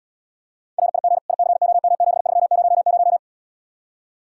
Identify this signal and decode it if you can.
Morse code, heard as a keyed tone, about 77 wpm, 700 Hz. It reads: RIG ICOM7300